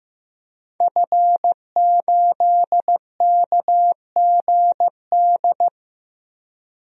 Morse code, heard as a keyed tone, about 15 wpm, 700 Hz. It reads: F8KGD
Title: DXCC-100 Call Signs - 5 Characters - 15wpm